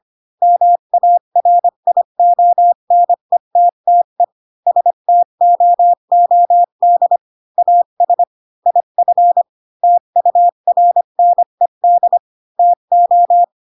Morse code, heard as a keyed tone, about 25 wpm, 700 Hz. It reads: MARIONETTE STOOD AS IF TURNED TO
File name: Pinocchio / Pinocchio ch5 20 25 ik1hge